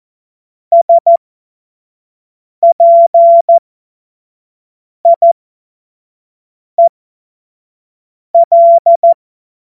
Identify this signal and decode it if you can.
Morse code, heard as a keyed tone, about 14 wpm, spaced as for 5 wpm, 680 Hz. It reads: SPIEL
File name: Morsekurs / Witze / Witz 09. Februar 2021, 25 F70